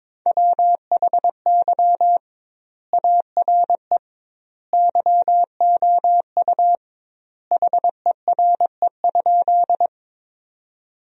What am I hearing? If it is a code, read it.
Morse code, heard as a keyed tone, about 22 wpm, 700 Hz. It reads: WHY ARE YOU HERE?
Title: 2x Word-Spacing - Sentences from Top 200 Words - 22wpm